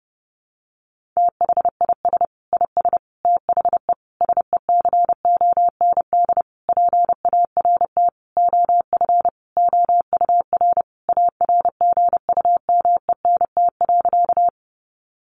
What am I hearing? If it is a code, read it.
Morse code, heard as a keyed tone, about 30 wpm, 700 Hz. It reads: THIS IS THE SECOND PART OF OUR ARGUMENT.
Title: Sentences from Top 1400 Words 30wpm